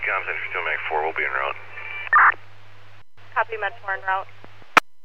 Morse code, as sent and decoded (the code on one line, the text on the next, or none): none